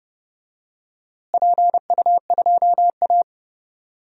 Morse code, heard as a keyed tone, about 32 wpm, 700 Hz. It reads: PU2A
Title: DXCC-100 Call Signs - 4 Characters - Repeated 2x - 30wpm